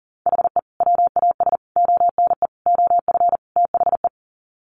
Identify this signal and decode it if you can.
Morse code, heard as a keyed tone, about 40 wpm, 700 Hz. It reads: HE WAS ONE OF THE
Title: Sets of 5 Words 40wpm